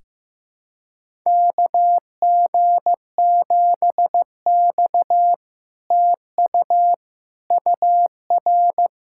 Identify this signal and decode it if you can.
Morse code, heard as a keyed tone, about 15 wpm, 700 Hz. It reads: KG7X TU UR
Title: SOTA QSO Practice 15wpm